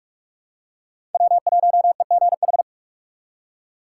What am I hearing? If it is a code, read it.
Morse code, heard as a keyed tone, about 47 wpm, 700 Hz. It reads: W1EGH
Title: USA Call Signs - 5 Characters - 4x Character Spacing - 45wpm